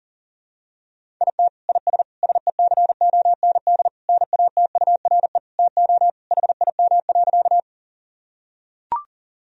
Morse code, heard as a keyed tone, about 40 wpm, 700 Hz. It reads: IT IS SECOND NATURE TO HIM.
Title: Sentences from Top 1000 Words - Review Part 1 - 40wpm